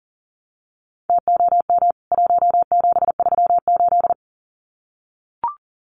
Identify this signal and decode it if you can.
Morse code, heard as a keyed tone, about 40 wpm, 700 Hz. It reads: TOM 1738
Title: CWT Exchange 40wpm